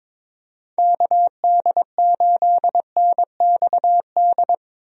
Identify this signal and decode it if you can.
Morse code, heard as a keyed tone, about 21 wpm, 700 Hz. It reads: KD8NXD
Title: USA Call Signs - 6 Characters - Repeated 3x - 22wpm